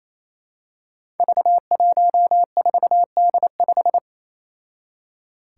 Morse code, heard as a keyed tone, about 27 wpm, 700 Hz. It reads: V14D5
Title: Five Challenging Letter-Number Combinations 28wpm